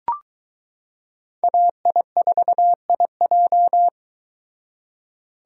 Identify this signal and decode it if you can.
Morse code, heard as a keyed tone, about 21 wpm, 700 Hz. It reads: AI4IJ